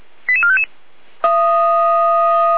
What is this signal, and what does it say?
Five-tone selective call (ZVEI), heard as quick runs of five tones, tones of about 70 ms. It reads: 88455